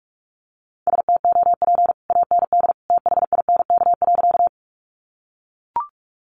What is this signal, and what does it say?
Morse code, heard as a keyed tone, about 44 wpm, 700 Hz. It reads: STOP AND THINK.
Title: Sentences from Top 300 Words 45wpm